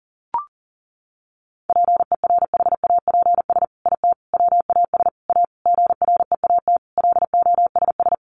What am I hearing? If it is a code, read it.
Morse code, heard as a keyed tone, about 40 wpm, 700 Hz. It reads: PERHAPS IT WAS A GREAT LOSS
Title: Sentences from Top 1400 Words 40wpm